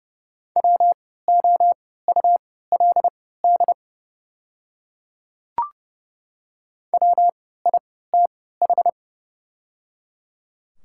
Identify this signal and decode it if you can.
Morse code, heard as a keyed tone, about 30 wpm, spaced as for 17 wpm, 690 Hz. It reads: WOULD WITH
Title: 3x Character Spacing - Top 200 Words - 30wpm